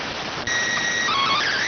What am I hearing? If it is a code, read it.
SSTV (slow-scan television), PD 290.